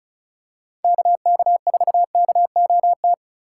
Morse code, heard as a keyed tone, about 36 wpm, 700 Hz. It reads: KK4KOT
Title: USA Call Signs - 6 Characters - 3x Character Spacing - 35wpm